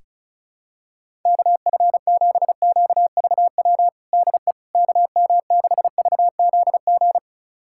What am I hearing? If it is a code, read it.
Morse code, heard as a keyed tone, about 36 wpm, 700 Hz. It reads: KF7QVW DE KM6VZG
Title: USA Callsign DE Callsign 35wpm